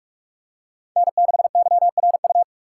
Morse code, heard as a keyed tone, about 39 wpm, 700 Hz. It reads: N6YRU